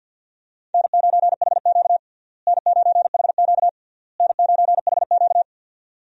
Morse code, heard as a keyed tone, about 46 wpm, 700 Hz. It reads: N9HQ N9HQ N9HQ
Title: USA Call Signs - 4 Characters - Repeated 3x - 50wpm